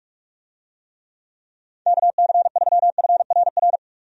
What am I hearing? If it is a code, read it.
Morse code, heard as a keyed tone, about 42 wpm, 700 Hz. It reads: KC3FRR
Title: USA Call Signs - 6 Characters - 45wpm